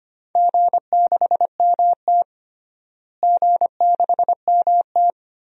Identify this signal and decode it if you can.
Morse code, heard as a keyed tone, about 25 wpm, 700 Hz. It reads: G6MT G6MT
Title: DXCC-100 Call Signs - 4 Characters - Repeated 3x - 25wpm